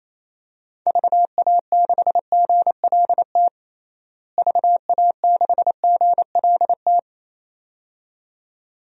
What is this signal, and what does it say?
Morse code, heard as a keyed tone, about 28 wpm, 700 Hz. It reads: VA6GLT VA6GLT
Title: DXCC-100 Call Signs - 6 Characters - Repeated 2x - 28wpm